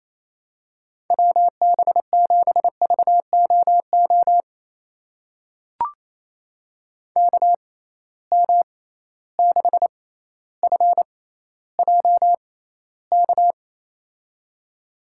Morse code, heard as a keyed tone, about 28 wpm, spaced as for 14 wpm, 700 Hz. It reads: WB7VOO KM6FJK
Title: USA Call Signs - 6x Character Spacing - 28wpm